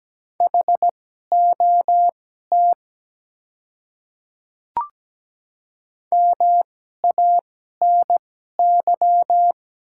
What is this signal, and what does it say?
Morse code, heard as a keyed tone, about 17 wpm, spaced as for 12 wpm, 700 Hz. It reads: HOT MANY
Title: 2x Character Spacing - Top 100 Words - 17wpm